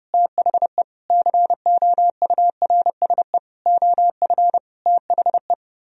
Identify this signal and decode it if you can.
Morse code, heard as a keyed tone, about 38 wpm, 700 Hz. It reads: THE COURSE OF THE